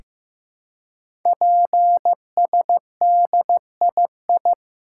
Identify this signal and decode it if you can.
Morse code, heard as a keyed tone, about 15 wpm, 700 Hz. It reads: PSDII
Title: Code Groups - 5 Letter:Numbers - 15wpm